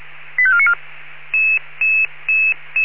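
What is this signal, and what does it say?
Five-tone selective call (ZVEI), heard as quick runs of five tones, tones of about 70 ms. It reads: 85494
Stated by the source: ZVEI five-tone selective call received by SDR.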